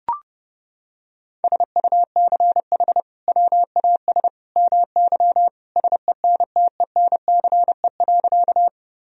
Morse code, heard as a keyed tone, about 32 wpm, 700 Hz. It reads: SUCH WAS MY SENTENCE.
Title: Sentences from Top 1000 Words - Review Part 1 - 30wpm